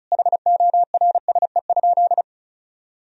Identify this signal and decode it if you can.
Morse code, heard as a keyed tone, about 34 wpm, 700 Hz. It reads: HORSE?
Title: Sentences from Top 1000 Words - Review Part 2 - 35wpm